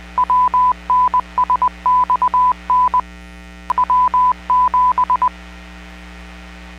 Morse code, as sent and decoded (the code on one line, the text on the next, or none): .-- -. ... -..- -. / .-- --...
WNSXN W7